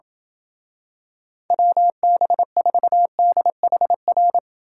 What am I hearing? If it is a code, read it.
Morse code, heard as a keyed tone, about 28 wpm, 700 Hz. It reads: WB4DHR